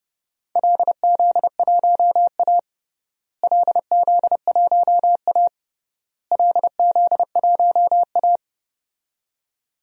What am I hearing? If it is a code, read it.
Morse code, heard as a keyed tone, about 32 wpm, 700 Hz. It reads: LZ1A LZ1A LZ1A
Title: DXCC-100 Call Signs - 4 Characters - Repeated 3x - 30wpm